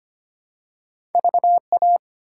Morse code, heard as a keyed tone, about 25 wpm, 700 Hz. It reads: VA
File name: practice / 25wpm / ARRL Sections v2 - 25wpm